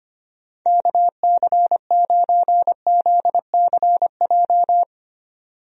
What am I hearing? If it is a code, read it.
Morse code, heard as a keyed tone, about 25 wpm, 700 Hz. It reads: KC9ZCJ
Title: USA Call Signs - 6 Characters - 25wpm